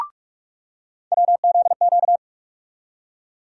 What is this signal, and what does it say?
Morse code, heard as a keyed tone, about 46 wpm, 700 Hz. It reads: WZQ
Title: USA Call Sign Postfixes - 3 Letters 45wpm